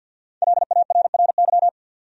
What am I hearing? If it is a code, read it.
Morse code, heard as a keyed tone, about 48 wpm, 700 Hz. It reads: LARRY
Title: USA Common Names - 50wpm